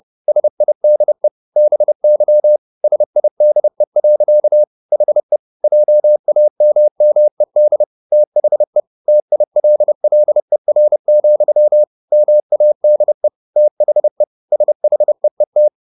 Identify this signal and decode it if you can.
Morse code, heard as a keyed tone, about 30 wpm, 600 Hz. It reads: SIDE BY SIDE. HE JAMMED THE TILLER, MADE THE SHEET